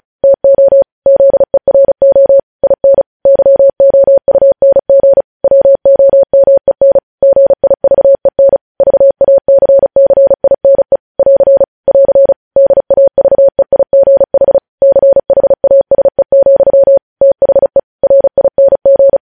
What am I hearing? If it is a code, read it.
Morse code, heard as a keyed tone, about 36 wpm, 550 Hz. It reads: TO ZERO IN YOUNG WOMEN GIVEN VACCINE <AR> <AR> DAVEIGH CHASE, THE RING
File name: podcast / News headlines-1x-35x35